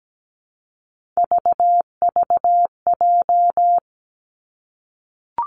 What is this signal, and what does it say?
Morse code, heard as a keyed tone, about 17 wpm, 700 Hz. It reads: VVJ
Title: Three Challenging Letter-Number Combinations 17wpm